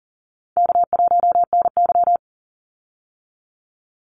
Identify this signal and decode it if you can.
Morse code, heard as a keyed tone, about 39 wpm, 700 Hz. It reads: K1NY
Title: Easy Call Signs Repeated 3x - 40wpm